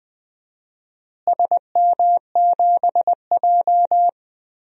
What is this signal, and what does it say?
Morse code, heard as a keyed tone, about 20 wpm, 700 Hz. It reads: SM7J
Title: DXCC-100 Call Signs - 4 Characters - Repeated 3x - 20wpm